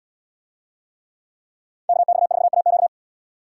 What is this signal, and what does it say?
Morse code, heard as a keyed tone, about 74 wpm, 690 Hz. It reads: KX4AQ